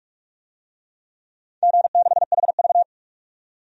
Morse code, heard as a keyed tone, about 44 wpm, 700 Hz. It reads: G6HV
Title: DXCC-100 Call Signs - 4 Characters - 45wpm